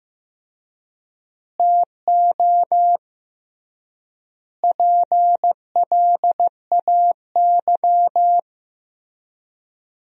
Morse code, heard as a keyed tone, about 16 wpm, 700 Hz. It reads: TO PLAY
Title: 3x Word-Spacing - Sentences from Top 300 Words - 15wpm